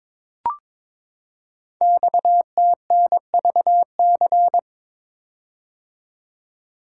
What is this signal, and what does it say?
Morse code, heard as a keyed tone, about 22 wpm, 700 Hz. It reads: XTNVC
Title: Code Groups - 5 Letter:Numbers - 22wpm